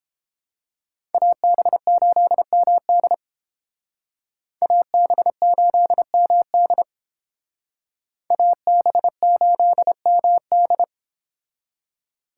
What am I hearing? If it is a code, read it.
Morse code, heard as a keyed tone, about 30 wpm, 700 Hz. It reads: AB8MD AB8MD AB8MD